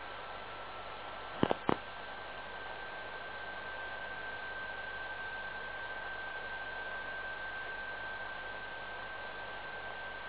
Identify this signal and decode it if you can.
PSK31, about 31.25 baud.